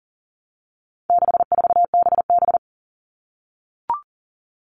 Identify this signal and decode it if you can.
Morse code, heard as a keyed tone, about 40 wpm, 700 Hz. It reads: 64BB
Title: All Cut Numbers - 4-Digit 40wpm